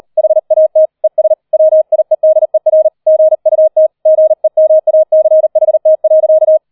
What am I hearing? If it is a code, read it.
Morse code, heard as a keyed tone, about 38 wpm, 630 Hz. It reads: HAT ES WIEDER GUT GEMACHT.